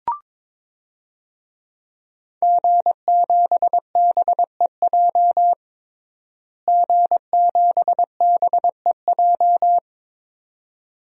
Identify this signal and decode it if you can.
Morse code, heard as a keyed tone, about 22 wpm, 700 Hz. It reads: G7BEJ G7BEJ
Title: DXCC-100 Call Signs - 5 Characters - Repeated 2x - 22wpm